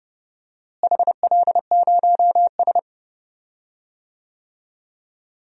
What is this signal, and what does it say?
Morse code, heard as a keyed tone, about 32 wpm, 700 Hz. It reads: HL0S
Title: DXCC-100 Call Signs - 4 Characters - Repeated 2x - 30wpm